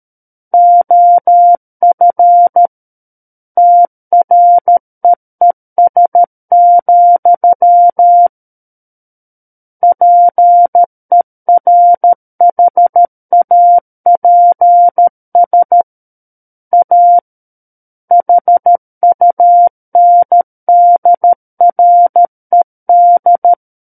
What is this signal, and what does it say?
Morse code, heard as a keyed tone, about 13 wpm, 700 Hz. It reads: OF TREES, PERHAPS A HUNDRED
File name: Book One / 04-Chapter 4 - 13 WPM